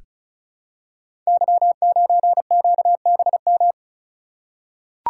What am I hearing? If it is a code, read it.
Morse code, heard as a keyed tone, about 36 wpm, 700 Hz. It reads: Y9QBM
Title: Code Groups - 5 Letter:Numbers - 35wpm